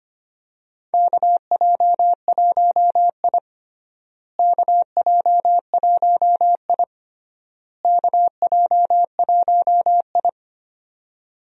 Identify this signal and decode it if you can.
Morse code, heard as a keyed tone, about 25 wpm, 700 Hz. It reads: KJ1I KJ1I KJ1I
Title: USA Call Signs - 4 Characters - Repeated 3x - 25wpm